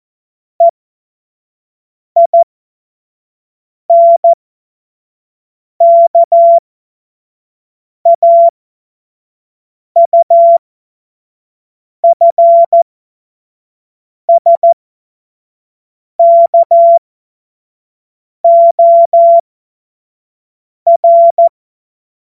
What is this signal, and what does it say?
Morse code, heard as a keyed tone, about 14 wpm, spaced as for 5 wpm, 680 Hz. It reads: EINKAUFSKOR
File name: Morsekurs / Witze / Witz 17. August 2021, 25 F70